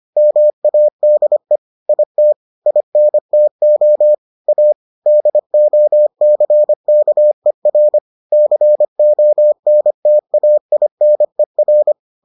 Morse code, heard as a keyed tone, about 25 wpm, 600 Hz. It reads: MADE IT INTO A DOCKER CONTAINER